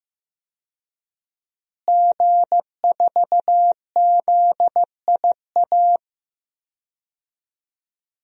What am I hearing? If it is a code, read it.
Morse code, heard as a keyed tone, about 16 wpm, 700 Hz. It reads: G4ZIA
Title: DXCC-100 Call Signs - 5 Characters - Repeated 3x - 15wpm